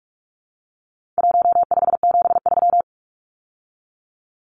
Morse code, heard as a keyed tone, about 44 wpm, 700 Hz. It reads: 1573
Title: Numbers - Contest Style - 4-Digit 45wpm